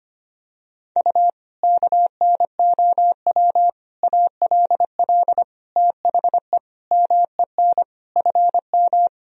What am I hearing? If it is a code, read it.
Morse code, heard as a keyed tone, about 25 wpm, 700 Hz. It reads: U KNOW ALL THE MEN FM
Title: Sentences from Top 300 Words and Abbreviations 25wpm